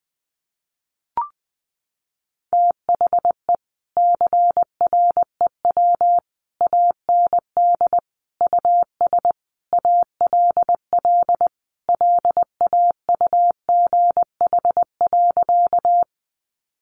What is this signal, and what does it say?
Morse code, heard as a keyed tone, about 20 wpm, 700 Hz. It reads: THE CREW AND US ALL LAUGH.
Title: Sentences from Top 1500 Words 20wpm